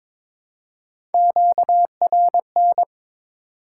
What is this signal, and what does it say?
Morse code, heard as a keyed tone, about 22 wpm, 700 Hz. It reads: QRN
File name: Top 90 QSO Elements - 3x Character Spacing - 22wpm